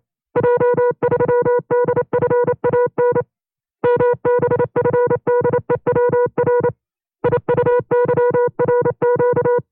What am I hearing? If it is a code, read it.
Morse code, heard as a keyed tone, about 27 wpm, 490 Hz. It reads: J3DFAN MBFDEWR IUYRQ
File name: FreeFiles / CW-practice / 25WPM-side1